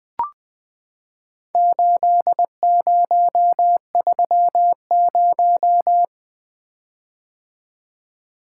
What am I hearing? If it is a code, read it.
Morse code, heard as a keyed tone, about 21 wpm, 700 Hz. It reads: 8030
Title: Four Letter-Number Combinations - 20wpm